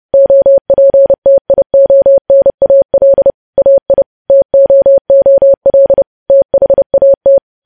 Morse code, heard as a keyed tone, about 30 wpm, 550 Hz. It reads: OPTIONAL AI TOOL THAT
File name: podcast / News headlines-1x-30x30